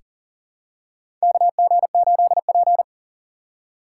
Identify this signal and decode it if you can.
Morse code, heard as a keyed tone, about 40 wpm, 700 Hz. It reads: KG8P
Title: USA Call Signs - 4 Characters - 3x Character Spacing - 40wpm